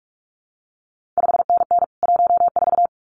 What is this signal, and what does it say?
Morse code, heard as a keyed tone, about 46 wpm, 700 Hz. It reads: 5NN 14
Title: CQ WW DX Contest Exchange - 45wpm